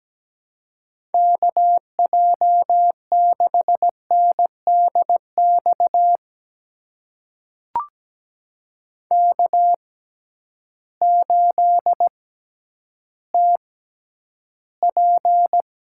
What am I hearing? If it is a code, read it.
Morse code, heard as a keyed tone, about 17 wpm, spaced as for 10 wpm, 700 Hz. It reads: KJ6NDX K8TP
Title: USA Call Signs - 6x Character Spacing - 17wpm